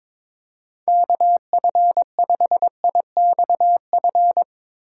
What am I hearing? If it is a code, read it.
Morse code, heard as a keyed tone, about 22 wpm, 700 Hz. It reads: KF5IXF
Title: USA Call Signs - 6 Characters - 2x Character Spacing - 22wpm